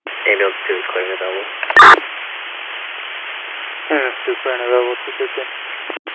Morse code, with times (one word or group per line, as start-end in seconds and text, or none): none